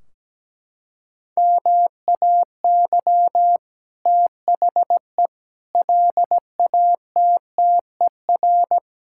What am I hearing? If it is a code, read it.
Morse code, heard as a keyed tone, about 17 wpm, 700 Hz. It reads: MAY THE LATTER
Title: Sentences from Top 1600 Words 17wpm